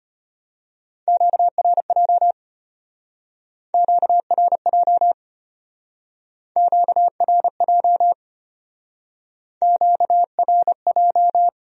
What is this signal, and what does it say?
Morse code, heard as a keyed tone, about 33 wpm, 700 Hz. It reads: QRJ QRJ QRJ QRJ